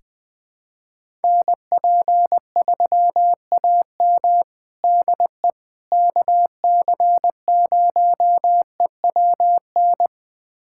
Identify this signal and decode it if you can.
Morse code, heard as a keyed tone, about 20 wpm, 700 Hz. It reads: NP3AM DE KC0EWN
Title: USA Callsign DE Callsign 20wpm